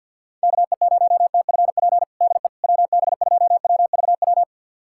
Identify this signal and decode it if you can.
Morse code, heard as a keyed tone, about 47 wpm, 690 Hz. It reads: KE0TFP DE WB2WVW
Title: USA Callsign DE Callsign - 6 Characters - 50wpm